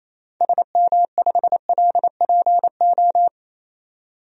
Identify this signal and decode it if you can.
Morse code, heard as a keyed tone, about 28 wpm, 700 Hz. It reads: SM5LPO
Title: DXCC-100 Call Signs - 6 Characters - 28wpm